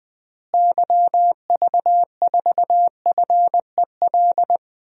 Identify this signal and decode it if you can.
Morse code, heard as a keyed tone, about 20 wpm, 700 Hz. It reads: YV4FEL
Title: DXCC-100 Call Signs - 6 Characters - Repeated 3x - 20wpm